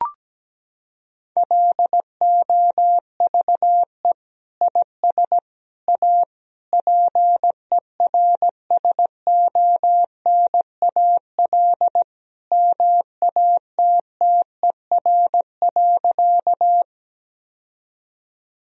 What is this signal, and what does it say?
Morse code, heard as a keyed tone, about 17 wpm, 700 Hz. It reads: LOVE IS A PERSONAL MATTER.
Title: Sentences from Top 1000 Words - Review Part 4 - 17wpm